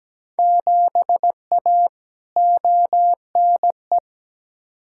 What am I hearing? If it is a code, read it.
Morse code, heard as a keyed tone, about 17 wpm, 700 Hz. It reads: 7A ONE
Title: Field Day Exchanges Encore - 17wpm